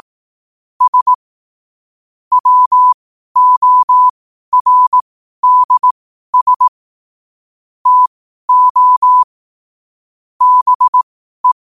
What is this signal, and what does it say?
Morse code, heard as a keyed tone, about 18 wpm, spaced as for 12 wpm, 1000 Hz. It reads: S WORDS TO BE
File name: qotdfiles / 12wpm